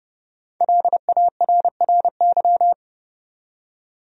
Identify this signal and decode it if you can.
Morse code, heard as a keyed tone, about 30 wpm, 700 Hz. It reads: LARRY
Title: USA Common Names - 30wpm